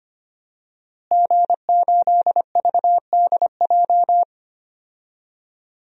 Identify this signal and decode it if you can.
Morse code, heard as a keyed tone, about 25 wpm, 700 Hz. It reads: G8VDJ
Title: DXCC-100 Call Signs - 5 Characters - Repeated 2x - 25wpm